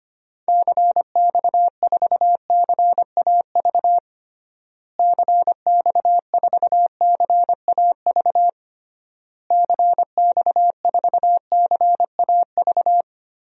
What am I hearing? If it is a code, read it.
Morse code, heard as a keyed tone, about 25 wpm, 700 Hz. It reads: CX4CAV CX4CAV CX4CAV
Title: DXCC-100 Call Signs - 6 Characters - Repeated 3x - 25wpm